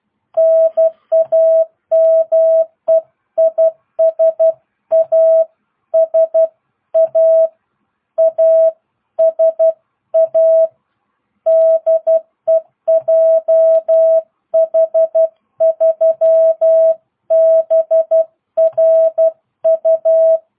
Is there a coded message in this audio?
Yes: Morse code from 0.4 to 20.4 s, heard as a keyed tone, about 12 wpm, 650 Hz. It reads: NAME IS ASA ASA DE JH3BRU